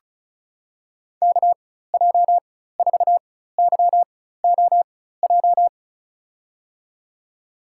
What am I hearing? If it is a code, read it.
Morse code, heard as a keyed tone, about 36 wpm, spaced as for 16 wpm, 700 Hz. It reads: KJ4YOJ
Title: USA Call Signs - 6 Characters - 4x Character Spacing - 35wpm